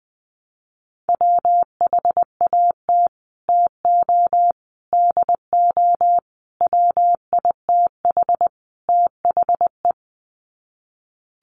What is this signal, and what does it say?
Morse code, heard as a keyed tone, about 20 wpm, 700 Hz. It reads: WHAT TO DO WITH THE
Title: Sets of 5 Words Encore2 20wpm